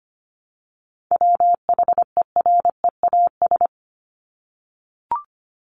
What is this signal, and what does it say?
Morse code, heard as a keyed tone, about 25 wpm, 700 Hz. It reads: WHEREAS
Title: Top 1501-1750 Words 25wpm